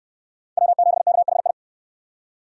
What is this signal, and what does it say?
Morse code, heard as a keyed tone, about 69 wpm, 700 Hz. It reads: W7KHI